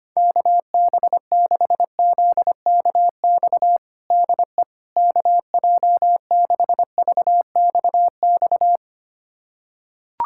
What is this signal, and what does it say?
Morse code, heard as a keyed tone, about 25 wpm, 700 Hz. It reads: KB6ZKX DE KJ6VXX
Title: USA Callsign DE Callsign - 6 Characters - 25wpm